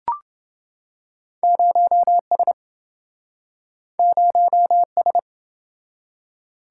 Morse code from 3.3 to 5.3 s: 0S

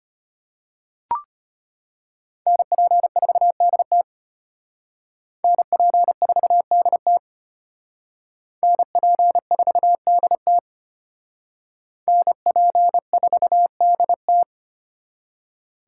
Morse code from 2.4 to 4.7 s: NP4DT

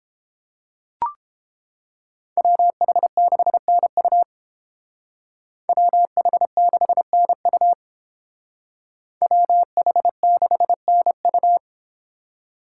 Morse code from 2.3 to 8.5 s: WH6NU WH6NU

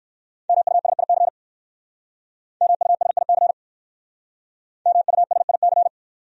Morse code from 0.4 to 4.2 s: MUSIC MUSIC

